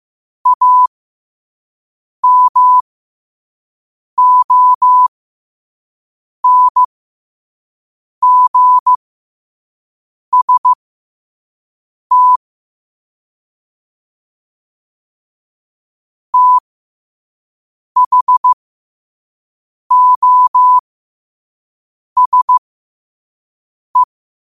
Morse code, heard as a keyed tone, about 15 wpm, spaced as for 5 wpm, 1000 Hz. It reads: AMONGST THOSE
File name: qotdfiles / 5wpm